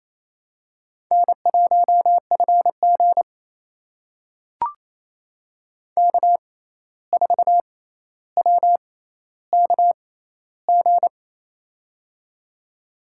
Morse code, heard as a keyed tone, about 28 wpm, spaced as for 13 wpm, 700 Hz. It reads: N1FG K4WKG
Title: USA Call Signs - 6x Character Spacing - 28wpm